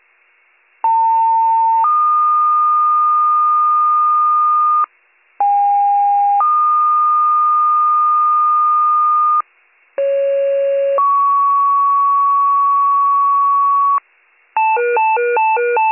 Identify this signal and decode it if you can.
Two-tone page, heard as a one-second tone, then a three-second tone, about 900 then 1220 Hz, about 810 then 1220 Hz, about 550 then 1120 Hz. It is heard from 0.8 to 14.0 s.